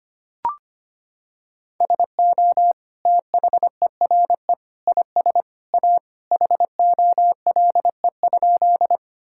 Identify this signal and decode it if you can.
Morse code, heard as a keyed tone, about 25 wpm, 700 Hz. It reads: SO THERE IS A HOLE?
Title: Sentences from Top 1600 Words 25wpm